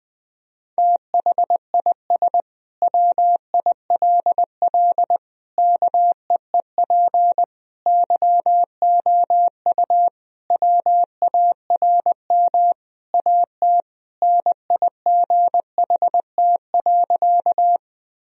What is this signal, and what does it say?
Morse code, heard as a keyed tone, about 20 wpm, 700 Hz. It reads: THIS WILL KEEP YOU WARM AT NIGHT.
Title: Sentences from Top 1000 Words - Review Part 2 - 20wpm